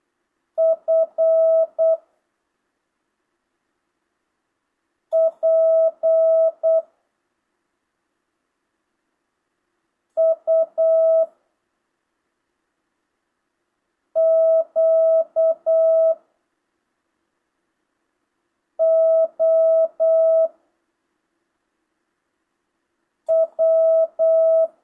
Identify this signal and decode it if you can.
Morse code, heard as a keyed tone, about 8 wpm, 660 Hz. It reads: F P U Q O W